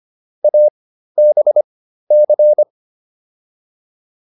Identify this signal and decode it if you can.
Morse code, heard as a keyed tone, about 25 wpm, spaced as for 13 wpm, 600 Hz. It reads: ABC